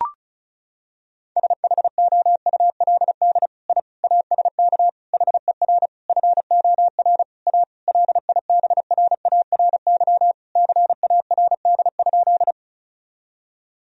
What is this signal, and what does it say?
Morse code, heard as a keyed tone, about 36 wpm, 700 Hz. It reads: SHOULD I ASK HER FOR A LIBRARY CARD?
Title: Sentences from Top 1000 Words - Review Part 4 - 35wpm